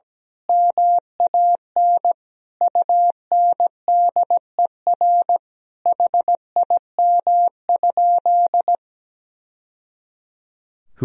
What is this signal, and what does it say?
Morse code, heard as a keyed tone, about 17 wpm, 700 Hz. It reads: MAN UNDER HIM?
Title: Sentences from Top 1000 Words - Review Part 1 - 17wpm